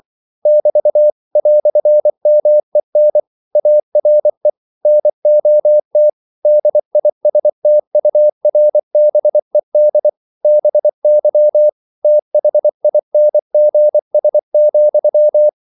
Morse code, heard as a keyed tone, about 24 wpm, 600 Hz. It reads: <BT> "MEN ARE NOT DISTURBED BY THINGS,
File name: CW / Practice / EnglishQuotations0324